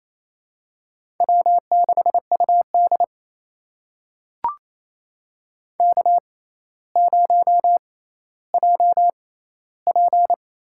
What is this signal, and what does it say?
Morse code, heard as a keyed tone, about 28 wpm, spaced as for 14 wpm, 700 Hz. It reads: W6UD K0JP
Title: USA Call Signs - 4 Characters - 6x Character Spacing - 28wpm